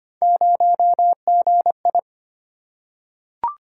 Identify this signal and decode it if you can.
Morse code, heard as a keyed tone, about 25 wpm, 700 Hz. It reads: 0GI
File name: Three Letter-Number Combinations - 25wpm